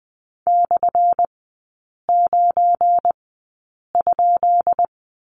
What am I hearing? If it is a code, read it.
Morse code, heard as a keyed tone, about 20 wpm, spaced as for 8 wpm, 700 Hz. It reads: /9?
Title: 178 Lesson 29e - Slash - Three Characters 20wpm